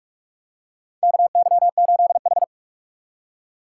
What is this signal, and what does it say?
Morse code, heard as a keyed tone, about 46 wpm, 690 Hz. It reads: KY8H
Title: USA Call Signs - 4 Characters - 8x Character Spacing - 45wpm